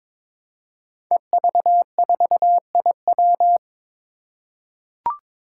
Morse code, heard as a keyed tone, about 22 wpm, 700 Hz. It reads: EV4IW
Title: Five Challenging Letter-Number Combinations 22wpm